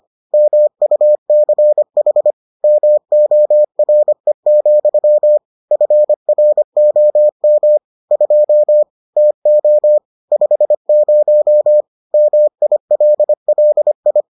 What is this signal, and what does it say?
Morse code, heard as a keyed tone, about 25 wpm, 600 Hz. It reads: MUCH MORE, FROM 2 TO 50 MILLI